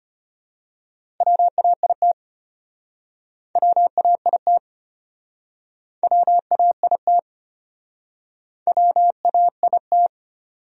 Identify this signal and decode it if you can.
Morse code, heard as a keyed tone, about 37 wpm, 700 Hz. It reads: WAIT WAIT WAIT WAIT